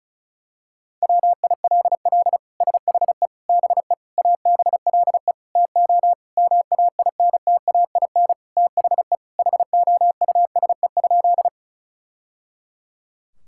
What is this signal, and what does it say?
Morse code, heard as a keyed tone, about 36 wpm, 700 Hz. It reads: WILL SHE BE ABLE TO MAINTAIN THE HOUSE?
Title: Sentences from Top 1000 Words - Review Part 5 - 35wpm